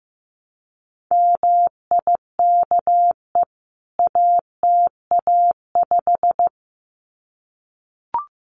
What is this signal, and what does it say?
Morse code, heard as a keyed tone, about 15 wpm, 700 Hz. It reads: MIKE ATA5
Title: CWT Exchange 15wpm